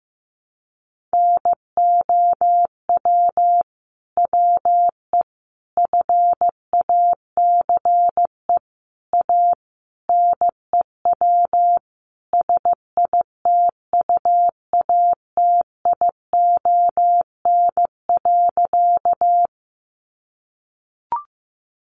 Morse code, heard as a keyed tone, about 15 wpm, 700 Hz. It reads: NOW WE FACE A NEW SITUATION.
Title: Sentences from Top 1200 Words 15wpm